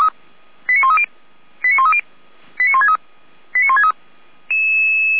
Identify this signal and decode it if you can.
Five-tone selective call (ZVEI), heard as quick runs of five tones, tones of about 70 ms. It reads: 89130 89130 89163 89163
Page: /455/SDR/ZVEI/